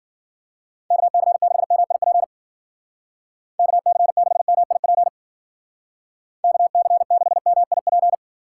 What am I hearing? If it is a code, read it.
Morse code, heard as a keyed tone, about 51 wpm, 700 Hz. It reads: KC6MIP KC6MIP KC6MIP